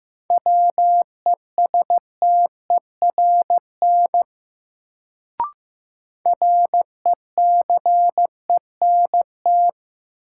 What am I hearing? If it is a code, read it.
Morse code, heard as a keyed tone, about 15 wpm, 700 Hz. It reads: WESTERN RECENT
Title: Sentences from Top 1000 Words - Review Part 7 - New Word Drill - 15wpm